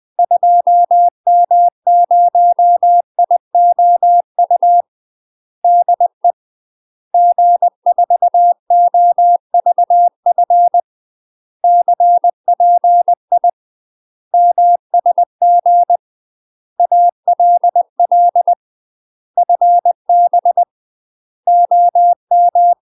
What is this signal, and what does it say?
Morse code, heard as a keyed tone, about 20 wpm, 700 Hz. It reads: 2M0IOU DE G4OVF CPI MSG ALL FB OM
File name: UK MASTER QSOs TRAINING / 20WPM / UK MASTER QSOs 0056